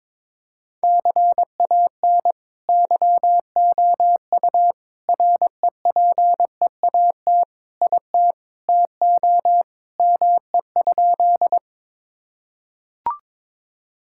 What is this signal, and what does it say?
Morse code, heard as a keyed tone, about 22 wpm, 700 Hz. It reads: CAN YOU REPEAT IT TO ME?
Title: Sentences from Top 1600 Words 22wpm